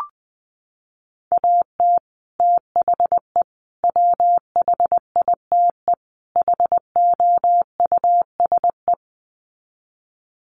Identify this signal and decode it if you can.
Morse code, heard as a keyed tone, about 20 wpm, 700 Hz. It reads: AT THE WHITE HOUSE
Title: Sets of 4 Words 20wpm